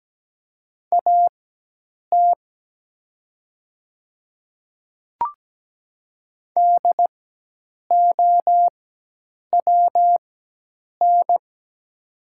Morse code, heard as a keyed tone, about 17 wpm, spaced as for 8 wpm, 700 Hz. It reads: AT DOWN
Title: 4x Character Spacing - Top 100 Words - 17wpm